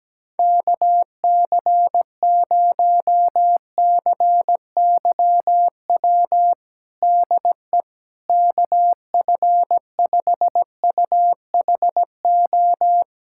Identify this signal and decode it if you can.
Morse code, heard as a keyed tone, about 17 wpm, 700 Hz. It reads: KC0CYW DE KF5UHO
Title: USA Callsign DE Callsign - 6 Characters - 17wpm